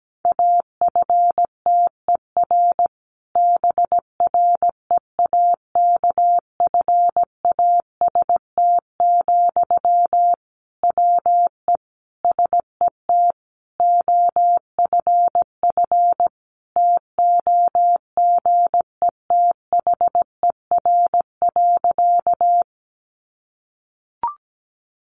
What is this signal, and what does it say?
Morse code, heard as a keyed tone, about 17 wpm, 700 Hz. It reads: AFTER BREAKFAST, WE SET OFF TOGETHER.
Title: Sentences from Top 1500 Words 17wpm